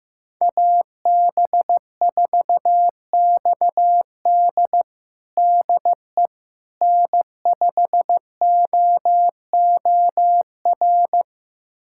Morse code, heard as a keyed tone, about 15 wpm, 700 Hz. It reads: AB4XD DE N5OOR
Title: USA Callsign DE Callsign - 5 Characters - 15wpm